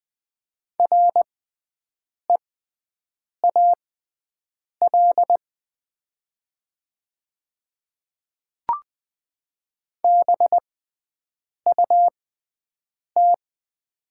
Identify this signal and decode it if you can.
Morse code, heard as a keyed tone, about 20 wpm, spaced as for 7 wpm, 700 Hz. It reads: REAL BUT
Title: 6x Character Spacing - Top 300 Words - 20wpm